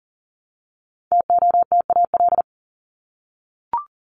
Morse code, heard as a keyed tone, about 40 wpm, 700 Hz. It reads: TOTAL
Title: Top 1251-1500 Words 40wpm